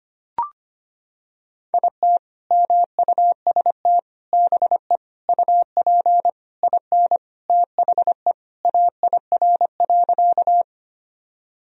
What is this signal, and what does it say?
Morse code, heard as a keyed tone, about 25 wpm, 700 Hz. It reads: IT MUST BE UP IN THE AIR.
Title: Sentences from Top 1000 Words - Review Part 1 - 25wpm